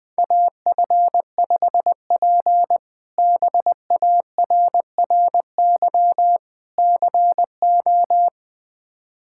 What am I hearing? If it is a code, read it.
Morse code, heard as a keyed tone, about 20 wpm, 700 Hz. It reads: AF5P BARRY CO
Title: K1USN SST QSO Practice - 20wpm